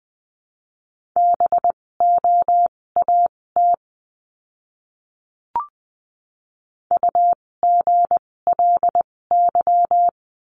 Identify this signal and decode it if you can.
Morse code, heard as a keyed tone, about 20 wpm, spaced as for 16 wpm, 700 Hz. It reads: BOAT UGLY
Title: 256 Lesson 42a - Words 12-20wpm